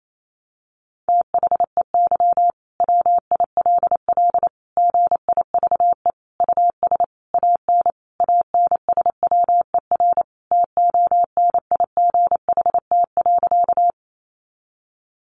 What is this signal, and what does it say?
Morse code, heard as a keyed tone, about 28 wpm, 700 Hz. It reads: THEY WILL GIVE US AN ANSWER TONIGHT.
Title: Sentences from Top 600 Words 28wpm